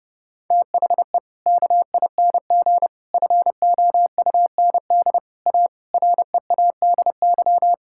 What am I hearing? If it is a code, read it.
Morse code, heard as a keyed tone, about 29 wpm, 700 Hz. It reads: THE KING FOUND A READY
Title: Sentences from Top 1000 Words - Review Part 5 - 30wpm